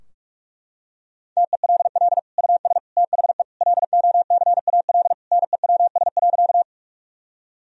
Morse code, heard as a keyed tone, about 43 wpm, 700 Hz. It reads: TELL US THE LOCAL NEWS.
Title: Sentences from Top 1000 Words - Review Part 3 - 45wpm